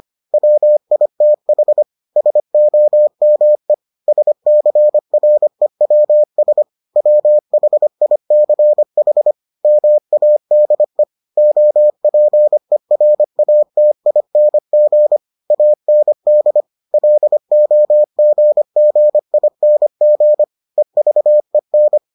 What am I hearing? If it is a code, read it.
Morse code, heard as a keyed tone, about 25 wpm, 600 Hz. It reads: WITH SOME SCREWS WHICH MADE OPERATING AND LOGGING EVEN